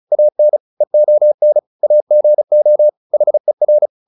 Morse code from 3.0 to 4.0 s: HER

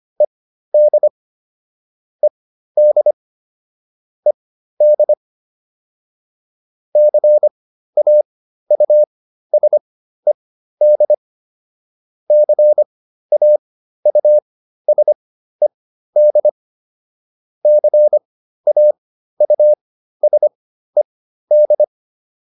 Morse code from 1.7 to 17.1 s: ED ED CAUSED CAUSED

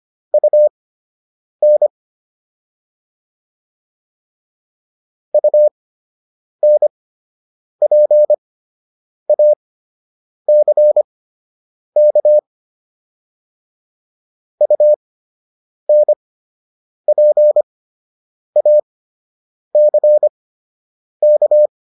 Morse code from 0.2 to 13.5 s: UN UNPACK

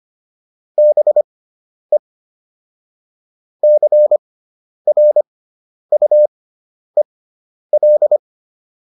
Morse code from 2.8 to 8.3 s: CRUEL

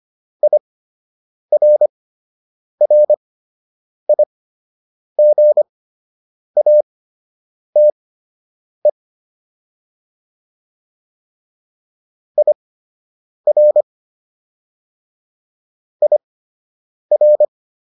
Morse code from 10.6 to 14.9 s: IR

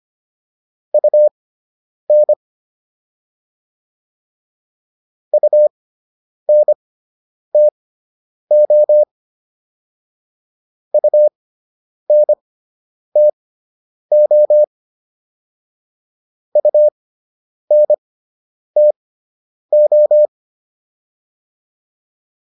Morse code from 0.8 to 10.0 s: UN UNTO

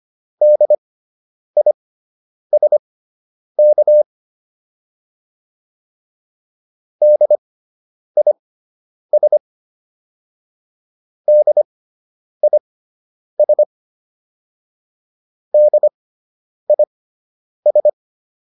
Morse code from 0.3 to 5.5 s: DISK

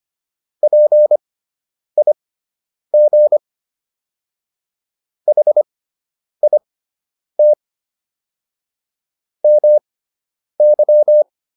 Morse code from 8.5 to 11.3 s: MY

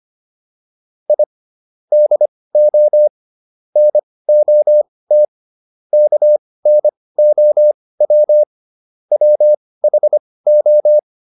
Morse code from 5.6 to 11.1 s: KNOW WHO

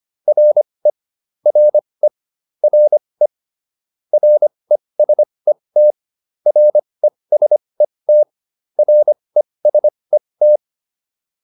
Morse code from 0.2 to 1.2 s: RE

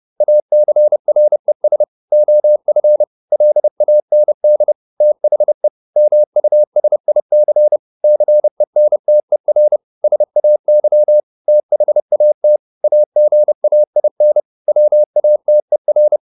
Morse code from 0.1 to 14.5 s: ACRES OF LAND THE MUSIC CENTER SAY THAT AGAIN

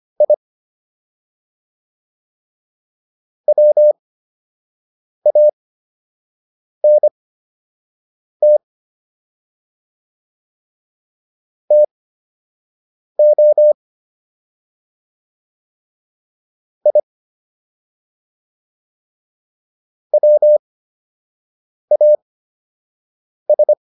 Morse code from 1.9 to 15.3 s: WANT TO

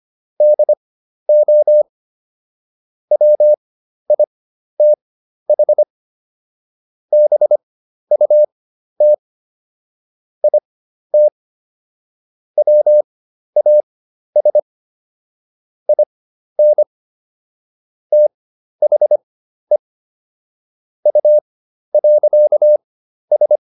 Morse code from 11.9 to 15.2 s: WAS

